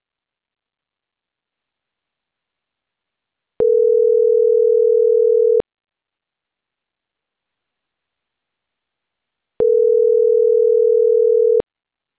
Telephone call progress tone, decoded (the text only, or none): ringback tone